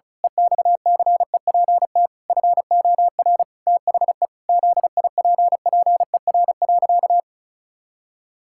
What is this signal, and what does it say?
Morse code, heard as a keyed tone, about 34 wpm, 700 Hz. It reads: EXCEPT FOR THE ZIPPER.
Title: Pangram Challenge v2 - 35wpm